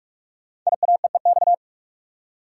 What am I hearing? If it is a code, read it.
Morse code, heard as a keyed tone, about 44 wpm, 700 Hz. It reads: IAEEX
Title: Code Groups - 5 Letter:Numbers - 45wpm